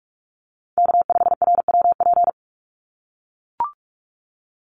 Morse code, heard as a keyed tone, about 44 wpm, 700 Hz. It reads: K5RWP
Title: NA SOTA Chaser Callsigns 45wpm